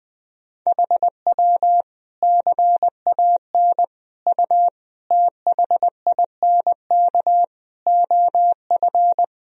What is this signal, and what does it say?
Morse code, heard as a keyed tone, about 21 wpm, 700 Hz. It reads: HW CAN U THINK OF
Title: Sentences from Top 200 Words and Abbreviations 20wpm